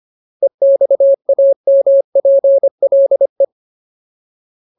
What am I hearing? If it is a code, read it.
Morse code, heard as a keyed tone, about 25 wpm, 550 Hz. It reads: EXAMPLE
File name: Ultimate Word List / 25WPM Ultimate